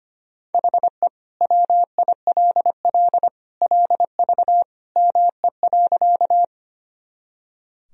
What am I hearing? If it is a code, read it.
Morse code, heard as a keyed tone, about 25 wpm, 700 Hz. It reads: HE WILL LV ME.
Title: Sentences from Top 400 Words and Abbreviations 25wpm